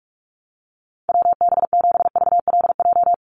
Morse code, heard as a keyed tone, about 44 wpm, 700 Hz. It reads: WB7VLJ
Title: USA Call Signs 45wpm